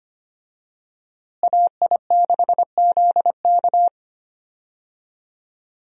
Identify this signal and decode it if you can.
Morse code, heard as a keyed tone, about 25 wpm, 700 Hz. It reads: AI6ZK
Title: USA Call Signs - 5 Characters - Repeated 3x - 25wpm